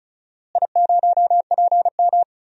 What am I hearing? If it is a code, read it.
Morse code, heard as a keyed tone, about 36 wpm, 700 Hz. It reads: I0PM
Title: DXCC-100 Call Signs - 4 Characters - Repeated 3x - 35wpm